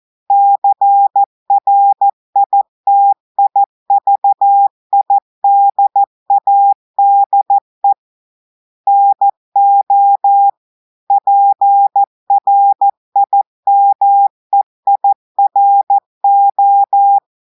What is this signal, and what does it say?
Morse code, heard as a keyed tone, about 14 wpm, 800 Hz. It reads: CRITIVIDADE NO PRIMEIRO
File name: audios / portugais / portugais texte en clair 1 vitesse 14 mots minutes